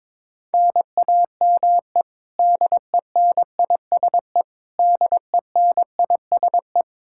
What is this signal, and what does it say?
Morse code, heard as a keyed tone, about 22 wpm, 700 Hz. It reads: NAME DENISE DENISE
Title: Practice QSO - Exchange 1 - Beginning - 22wpm